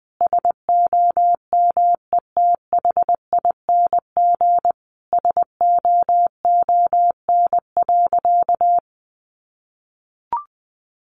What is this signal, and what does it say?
Morse code, heard as a keyed tone, about 20 wpm, 700 Hz. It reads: SOMETHING SOON.
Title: Sentences from Top 1500 Words 20wpm